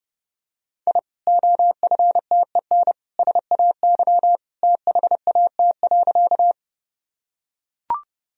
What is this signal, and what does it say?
Morse code, heard as a keyed tone, about 30 wpm, 700 Hz. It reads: I OFTEN SAY THAT.
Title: Sentences from Top 1000 Words - Review Part 1 - 30wpm